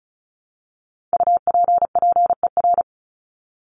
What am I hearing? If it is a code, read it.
Morse code, heard as a keyed tone, about 34 wpm, 700 Hz. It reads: UPPER
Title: Top 1751-2000 Words 35wpm